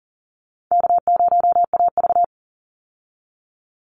Morse code, heard as a keyed tone, about 40 wpm, 700 Hz. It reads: K0AV
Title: Call Signs Repeated 2x - 40wpm